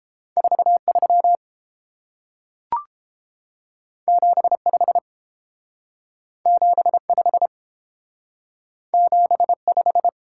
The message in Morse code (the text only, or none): 43 75 75 75